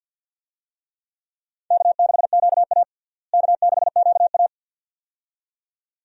Morse code, heard as a keyed tone, about 52 wpm, 700 Hz. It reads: K6QA K6QA
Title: USA Call Signs - 4 Characters - Repeated 2x - 50wpm